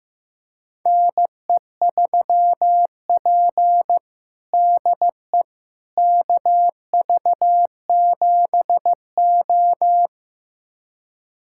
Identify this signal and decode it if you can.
Morse code, heard as a keyed tone, about 15 wpm, 700 Hz. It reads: NE3P DE KV7O
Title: USA Callsign DE Callsign - 4 Characters - 15wpm